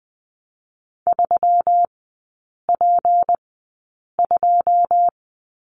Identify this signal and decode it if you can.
Morse code, heard as a keyed tone, about 20 wpm, spaced as for 8 wpm, 700 Hz. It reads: 3P2
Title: 172 Lesson 28e - Number 9 - Three Characters 20wpm